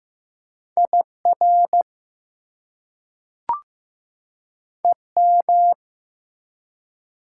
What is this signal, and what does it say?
Morse code, heard as a keyed tone, about 15 wpm, 700 Hz. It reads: IR EM
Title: Prefixes and Suffixes 15wpm